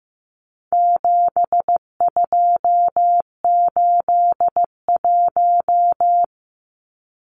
Morse code, heard as a keyed tone, about 15 wpm, 700 Hz. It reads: 7281
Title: Numbers - 4-Digit 15wpm